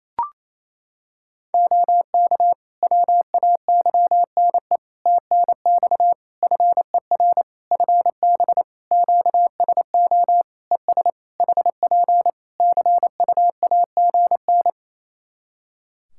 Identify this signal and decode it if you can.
Morse code, heard as a keyed tone, about 27 wpm, 700 Hz. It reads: OK WAYNE TNX FER FB QSO ES HP CUAGN
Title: Practice QSO - Exchange 4 - Ending - 28wpm